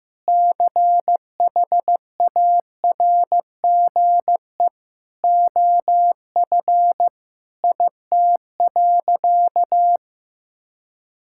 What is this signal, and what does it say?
Morse code, heard as a keyed tone, about 15 wpm, 700 Hz. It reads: CHARGE OF IT.
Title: Sentences from Top 1000 Words - Review Part 6 - 15wpm